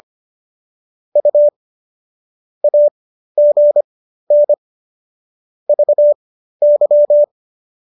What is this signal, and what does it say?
Morse code, heard as a keyed tone, about 25 wpm, spaced as for 13 wpm, 590 Hz. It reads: U AGN VY